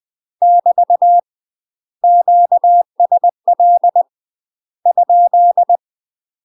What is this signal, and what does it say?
Morse code, heard as a keyed tone, about 20 wpm, 700 Hz. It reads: <BT> QSL ?
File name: UK MASTER QSOs TRAINING / 20WPM / UK MASTER QSOs 0070